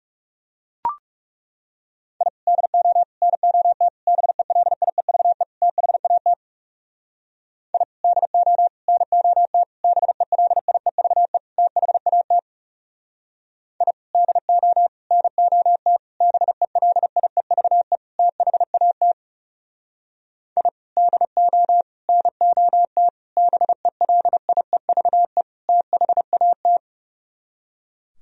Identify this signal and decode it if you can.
Morse code, heard as a keyed tone, about 37 wpm, 700 Hz. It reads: I DO NOT BELIEVE THAT I DO NOT BELIEVE THAT I DO NOT BELIEVE THAT I DO NOT BELIEVE THAT
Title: Speed-Racing - Sets of 5 Words 30wpm